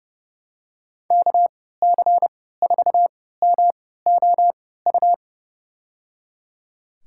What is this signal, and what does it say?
Morse code, heard as a keyed tone, about 30 wpm, spaced as for 17 wpm, 700 Hz. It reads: KC4MOU
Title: USA Call Signs - 6 Characters - 3x Character Spacing - 30wpm